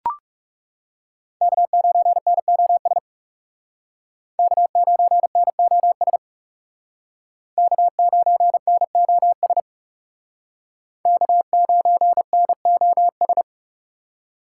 Morse code, heard as a keyed tone, about 36 wpm, 700 Hz. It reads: K9NOS K9NOS K9NOS K9NOS
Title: Speed Racing - USA 1x3 Call Signs 30wpm